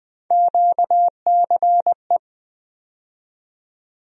Morse code, heard as a keyed tone, about 21 wpm, 700 Hz. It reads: QCE
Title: USA Call Sign Postfixes - 3 Letters 20wpm